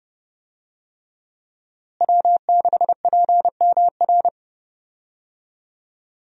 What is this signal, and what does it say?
Morse code, heard as a keyed tone, about 30 wpm, 700 Hz. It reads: W6PMR
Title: USA Call Signs - 5 Characters - Repeated 3x - 30wpm